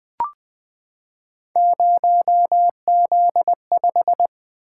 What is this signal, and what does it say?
Morse code, heard as a keyed tone, about 20 wpm, 700 Hz. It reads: 0Z5
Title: Three Letter-Number Combinations - 20wpm